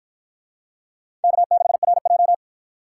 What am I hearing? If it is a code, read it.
Morse code, heard as a keyed tone, about 53 wpm, 700 Hz. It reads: K6RJ